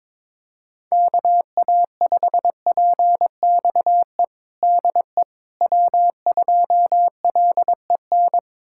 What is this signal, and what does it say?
Morse code, heard as a keyed tone, about 22 wpm, 700 Hz. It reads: KA5PXE DE W2LEN
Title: USA Callsign DE Callsign 22wpm